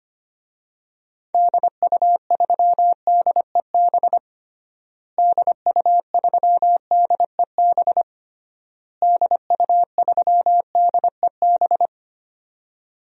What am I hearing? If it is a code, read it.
Morse code, heard as a keyed tone, about 25 wpm, 700 Hz. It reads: DU3DEB DU3DEB DU3DEB
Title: DXCC-100 Call Signs - 6 Characters - Repeated 3x - 25wpm